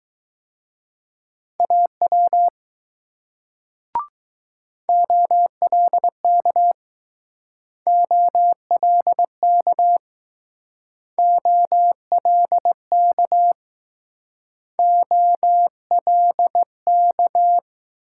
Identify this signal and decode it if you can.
Morse code, heard as a keyed tone, about 20 wpm, 700 Hz. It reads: AW OLK OLK OLK OLK